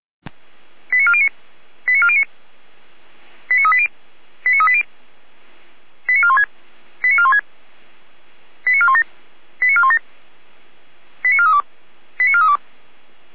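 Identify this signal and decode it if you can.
Five-tone selective call (ZVEI), heard as quick runs of five tones, tones of about 70 ms. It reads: 89409 89409 89380 89380 89416 89416 89417 89417 89432 89432
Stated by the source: ZVEI five-tone selective call 89175 received on an SDR.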